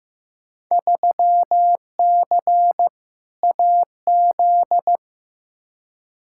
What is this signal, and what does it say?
Morse code, heard as a keyed tone, about 15 wpm, 700 Hz. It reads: 3C AZ
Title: Field Day Exchanges Encore - 15wpm